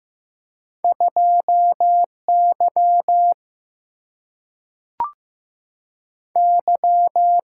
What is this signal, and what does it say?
Morse code, heard as a keyed tone, about 15 wpm, 700 Hz. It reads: 2 Y Y
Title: Single-Letter Single-Number 15wpm